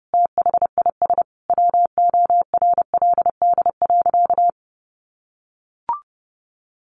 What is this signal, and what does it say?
Morse code, heard as a keyed tone, about 30 wpm, 700 Hz. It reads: THIS WORLD.
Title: Sentences from Top 600 Words 30wpm